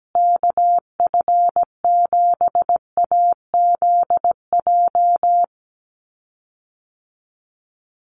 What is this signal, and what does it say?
Morse code, heard as a keyed tone, about 17 wpm, 700 Hz. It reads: KF7AZJ
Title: USA Call Signs Repeated 3x 17wpm